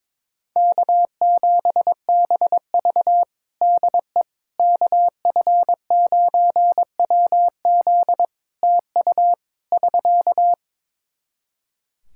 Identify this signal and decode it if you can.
Morse code, heard as a keyed tone, about 22 wpm, 700 Hz. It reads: K7BV DE KF9WZ TU <SK>
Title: K1USN SST QSO Practice 22wpm